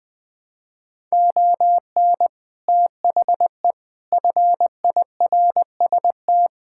Morse code, heard as a keyed tone, about 20 wpm, 700 Hz. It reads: ON THE FIRST
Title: Sets of 6 Words - 20wpm